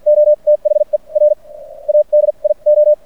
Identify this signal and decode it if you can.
Morse code, heard as a keyed tone, about 50 wpm, spaced as for 30 wpm, 590 Hz. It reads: O THE RADIO
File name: archived / CRSnet